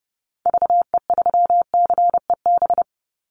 Morse code, heard as a keyed tone, about 30 wpm, 700 Hz. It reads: VE3CEB
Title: Call Signs - 30wpm